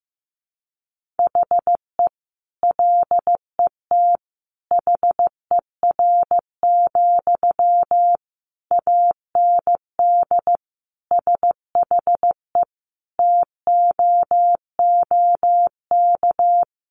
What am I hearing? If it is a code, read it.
Morse code, heard as a keyed tone, about 15 wpm, 700 Hz. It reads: HE LET HER, AND SHE TOOK
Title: Sentences from Top 1500 Words 15wpm